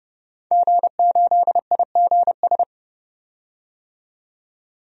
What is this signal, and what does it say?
Morse code, heard as a keyed tone, about 30 wpm, 700 Hz. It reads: G8IGS
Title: DXCC-100 Call Signs - 5 Characters - 30wpm